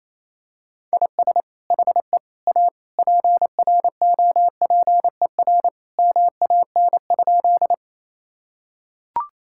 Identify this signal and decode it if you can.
Morse code, heard as a keyed tone, about 28 wpm, 700 Hz. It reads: IS HE A PROPER MAN?
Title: Sentences from Top 1000 Words - Review Part 5 - 28wpm